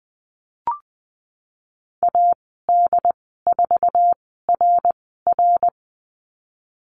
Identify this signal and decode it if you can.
Morse code, heard as a keyed tone, about 20 wpm, spaced as for 14 wpm, 700 Hz. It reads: AD4RR
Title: 076 Lesson 13g - Letter L - Call Signs 20wpm